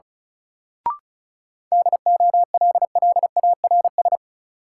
Morse code, heard as a keyed tone, about 36 wpm, 700 Hz. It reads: DOLLARS
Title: Sets of 5 Words - New Word Drill - 35wpm